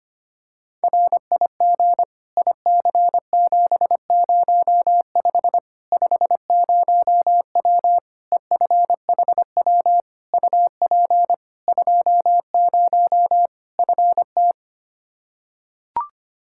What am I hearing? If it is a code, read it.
Morse code, heard as a keyed tone, about 25 wpm, 700 Hz. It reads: RIG IC705 50W EFHW UP 20 FT
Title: Practice QSO - Exchange 2 - Info - 25wpm